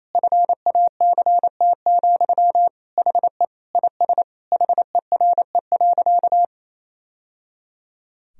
Morse code, heard as a keyed tone, about 28 wpm, 700 Hz. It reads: FACT, HE IS HERE.
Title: Sentences from Top 1000 Words - Review Part 3 - 28wpm